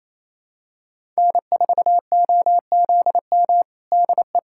Morse code, heard as a keyed tone, about 28 wpm, 700 Hz. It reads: N4OZM DE
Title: USA Callsign DE Callsign - 5 Characters - 28wpm